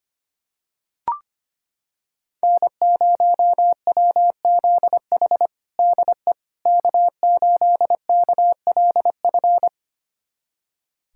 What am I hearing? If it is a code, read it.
Morse code, heard as a keyed tone, about 25 wpm, 700 Hz. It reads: N0WZH DE K8KLF
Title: USA Callsign DE Callsign - 5 Characters - 25wpm